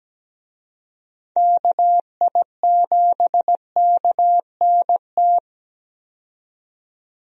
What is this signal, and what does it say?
Morse code, heard as a keyed tone, about 17 wpm, 700 Hz. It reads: KI7KNT
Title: USA Call Signs - 6 Characters - 17wpm